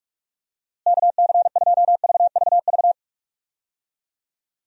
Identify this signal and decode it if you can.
Morse code, heard as a keyed tone, about 46 wpm, 700 Hz. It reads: KC2VVV
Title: USA Call Signs - 6 Characters - Repeated 2x - 45wpm